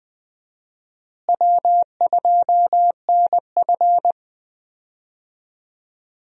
Morse code, heard as a keyed tone, about 21 wpm, 700 Hz. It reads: W2NF
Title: USA Call Signs - 4 Characters - Repeated 3x - 20wpm